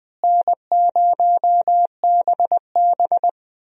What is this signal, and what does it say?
Morse code, heard as a keyed tone, about 20 wpm, 700 Hz. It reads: N0BB
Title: USA Call Signs - 4 Characters - Repeated 3x - 20wpm